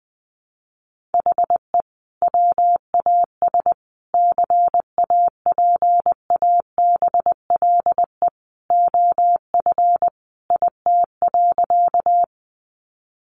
Morse code, heard as a keyed tone, about 20 wpm, 700 Hz. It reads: HE WAS CAPABLE OF IT.
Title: Sentences from Top 1200 Words 20wpm